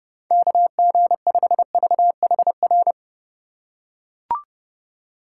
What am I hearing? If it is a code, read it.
Morse code, heard as a keyed tone, about 30 wpm, 700 Hz. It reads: KG5VHR
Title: USA Call Signs - 6 Characters - 3x Character Spacing - 30wpm